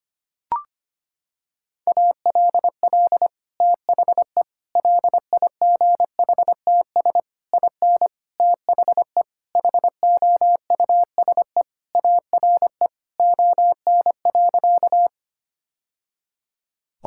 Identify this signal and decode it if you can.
Morse code, heard as a keyed tone, about 25 wpm, 700 Hz. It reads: ALL THE LIGHTS IN THE HOUSE ARE ON.
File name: practice / 25wpm / Sentences with 2-Word Phrases - 25wpm